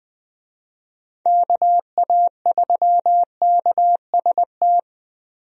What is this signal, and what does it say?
Morse code, heard as a keyed tone, about 21 wpm, 700 Hz. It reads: KA3KST
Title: USA Call Signs - 6 Characters - Repeated 3x - 20wpm